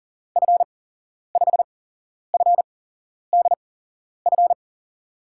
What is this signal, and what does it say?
Morse code, heard as a keyed tone, about 40 wpm, spaced as for 11 wpm, 700 Hz. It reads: F5FDF
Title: USA Call Signs - 6 Characters - 8x Character Spacing - 40wpm